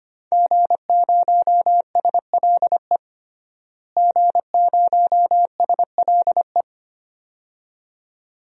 Morse code, heard as a keyed tone, about 25 wpm, 700 Hz. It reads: G0SLE G0SLE
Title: DXCC-100 Call Signs - 5 Characters - Repeated 2x - 25wpm